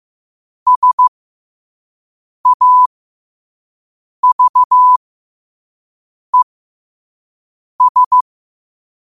Morse code, heard as a keyed tone, about 15 wpm, spaced as for 5 wpm, 1000 Hz. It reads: SAVES
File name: qotdfiles / 5wpm